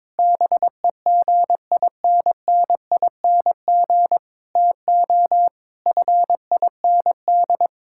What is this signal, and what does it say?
Morse code, heard as a keyed tone, about 22 wpm, 700 Hz. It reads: BEGINNING TO FIND
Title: Sentences from Top 1000 Words - Review Part 5 - 22wpm